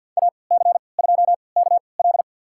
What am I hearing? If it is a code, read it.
Morse code, heard as a keyed tone, about 51 wpm, spaced as for 28 wpm, 700 Hz. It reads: AC2KL
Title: USA Call Signs - 3x Character Spacing - 50wpm